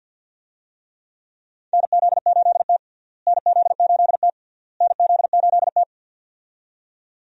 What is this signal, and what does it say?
Morse code, heard as a keyed tone, about 52 wpm, 700 Hz. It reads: NZ8T NZ8T NZ8T
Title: USA Call Signs - 4 Characters - Repeated 3x - 50wpm